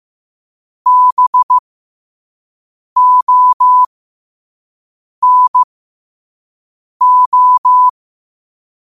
Morse code, heard as a keyed tone, about 15 wpm, spaced as for 5 wpm, 1000 Hz. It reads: BONO
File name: qotdfiles / 5wpm